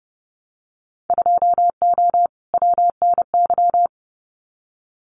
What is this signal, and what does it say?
Morse code, heard as a keyed tone, about 30 wpm, 700 Hz. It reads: 2O WNY
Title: Winter Field Day Exchanges Encore 30wpm